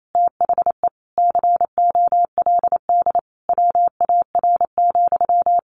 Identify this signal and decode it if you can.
Morse code, heard as a keyed tone, about 28 wpm, 700 Hz. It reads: THE COLD WAR,
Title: Sentences from Top 1200 Words 28wpm